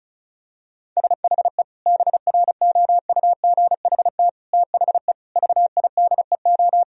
Morse code, heard as a keyed tone, about 34 wpm, 700 Hz. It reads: SHE BROUGHT THE VIDEO
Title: Sentences from Top 1000 Words - Review Part 4 - 35wpm